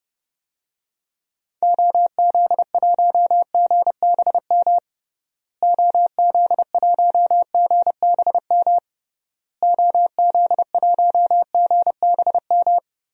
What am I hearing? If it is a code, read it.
Morse code, heard as a keyed tone, about 30 wpm, 700 Hz. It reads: OZ1GBM OZ1GBM OZ1GBM
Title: DXCC-100 Call Signs - 6 Characters - Repeated 3x - 30wpm